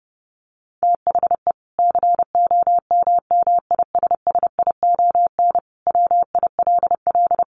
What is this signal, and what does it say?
Morse code, heard as a keyed tone, about 30 wpm, 700 Hz. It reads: THE COMMISSION WILL
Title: Sentences from Top 1400 Words 30wpm